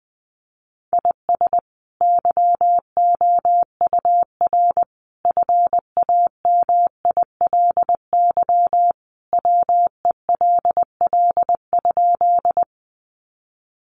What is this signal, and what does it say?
Morse code, heard as a keyed tone, about 20 wpm, 700 Hz. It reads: IS YOUR FAMILY WELL?
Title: Sentences from Top 400 Words 20wpm